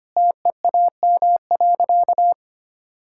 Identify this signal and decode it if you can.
Morse code, heard as a keyed tone, about 25 wpm, 700 Hz. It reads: TEAM.
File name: practice / 25wpm / Sentences with 3-Word Phrases - 25wpm